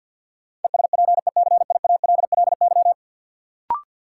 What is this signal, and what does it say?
Morse code, heard as a keyed tone, about 52 wpm, 700 Hz. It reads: ESPECIALLY
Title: Sentences from Top 700 Words - New Word Drill - 50wpm